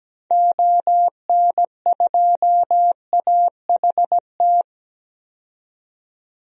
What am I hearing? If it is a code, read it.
Morse code, heard as a keyed tone, about 17 wpm, 700 Hz. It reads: ON2AHT
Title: DXCC-100 Call Signs - 6 Characters - 17wpm